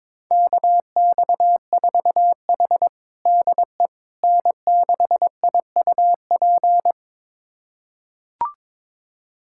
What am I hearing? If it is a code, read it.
Morse code, heard as a keyed tone, about 22 wpm, 700 Hz. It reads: KX4H DE N6IUP
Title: USA Callsign DE Callsign 22wpm